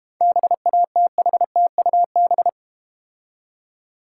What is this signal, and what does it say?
Morse code, heard as a keyed tone, about 32 wpm, 700 Hz. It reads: BATHTUB